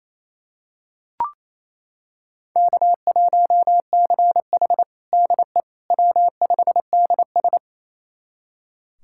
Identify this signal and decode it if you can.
Morse code, heard as a keyed tone, about 28 wpm, 700 Hz. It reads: K1CH DE W5DS
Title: USA Callsign DE Callsign - 4 Characters - 28wpm